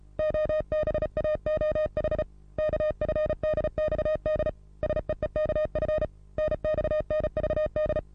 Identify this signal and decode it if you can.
Morse code, heard as a keyed tone, about 37 wpm, 620 Hz. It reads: OBAOH KFDXD SEEKF NXNVD
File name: repos / Morsekurs / DJ4UF Morsen27